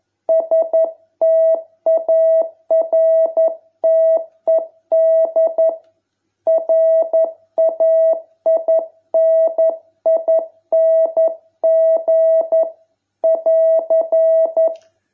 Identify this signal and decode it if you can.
Morse code, heard as a keyed tone, about 11 wpm, 660 Hz. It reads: STARTED RAINING <AR>